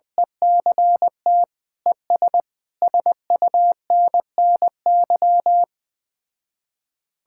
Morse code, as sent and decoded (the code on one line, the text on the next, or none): . -.-. - / . ... / ... ..- -. -. -.--
ECT ES SUNNY